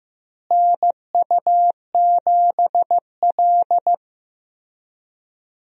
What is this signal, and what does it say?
Morse code, heard as a keyed tone, about 15 wpm, 700 Hz. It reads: NU7L
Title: USA Call Signs - 4 Characters - Repeated 3x - 15wpm